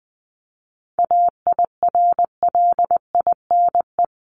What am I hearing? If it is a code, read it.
Morse code, heard as a keyed tone, about 20 wpm, 700 Hz. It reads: AIRLINE
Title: Top 1501-1750 Words 20wpm